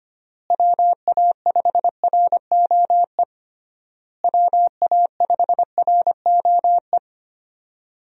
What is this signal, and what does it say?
Morse code, heard as a keyed tone, about 25 wpm, 700 Hz. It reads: WA5ROE WA5ROE
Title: USA Call Signs - 6 Characters - Repeated 3x - 25wpm